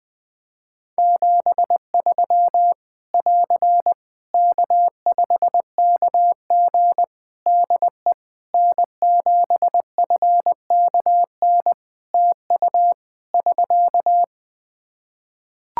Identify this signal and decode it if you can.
Morse code, heard as a keyed tone, about 20 wpm, 700 Hz. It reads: 73 <AR> K5KG DE N7FKN TU <SK>
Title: K1USN SST QSO Practice 20wpm